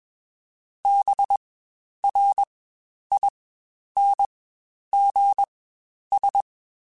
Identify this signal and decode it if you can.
Morse code, heard as a keyed tone, about 21 wpm, spaced as for 10 wpm, 800 Hz. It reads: BRINGS
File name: Quotes Spaced / Quotes Spaced-mp3s / Persperation sp @20